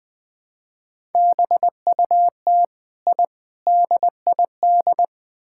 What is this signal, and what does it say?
Morse code, heard as a keyed tone, about 20 wpm, 700 Hz. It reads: BUT I DID
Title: Build A Phrase - 4-Word Phrases - 20wpm